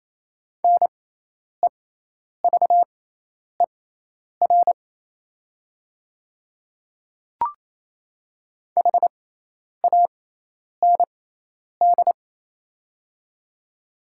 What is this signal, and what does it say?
Morse code, heard as a keyed tone, about 28 wpm, spaced as for 10 wpm, 700 Hz. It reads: NEVER HAND
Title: 6x Character Spacing - Top 500 Words - 28wpm